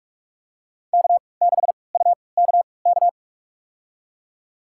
Morse code, heard as a keyed tone, about 43 wpm, spaced as for 25 wpm, 700 Hz. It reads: K6UKK
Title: USA Call Signs - 5 Characters - 3x Character Spacing - 45wpm